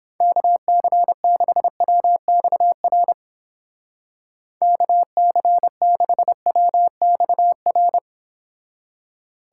Morse code, heard as a keyed tone, about 29 wpm, 700 Hz. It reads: KC6WXR KC6WXR